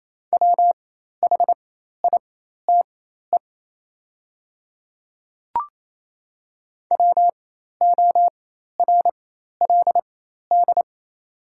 Morse code, heard as a keyed tone, about 28 wpm, spaced as for 13 wpm, 700 Hz. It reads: WHITE WORLD
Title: 4x Character Spacing - Top 500 Words - 28wpm